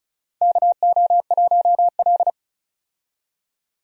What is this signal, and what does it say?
Morse code, heard as a keyed tone, about 34 wpm, 700 Hz. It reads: KO1L
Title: USA Call Signs - 4 Characters - 35wpm